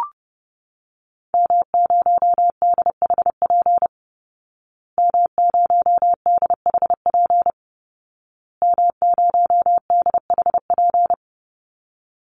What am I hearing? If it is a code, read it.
Morse code, heard as a keyed tone, about 28 wpm, 700 Hz. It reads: M0DHP M0DHP M0DHP
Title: Call Signs Repeated 2x - 30wpm